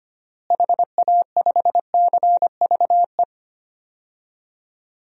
Morse code, heard as a keyed tone, about 25 wpm, 700 Hz. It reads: HA5CVE
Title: DXCC-100 Call Signs - 6 Characters - Repeated 2x - 25wpm